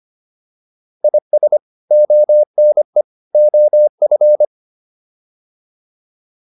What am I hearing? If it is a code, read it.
Morse code, heard as a keyed tone, about 25 wpm, 600 Hz. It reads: IS ONE OF